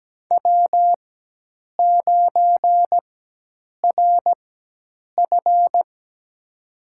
Morse code, heard as a keyed tone, about 17 wpm, spaced as for 8 wpm, 700 Hz. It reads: W9RF
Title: USA Call Signs - 4 Characters - 4x Character Spacing - 17wpm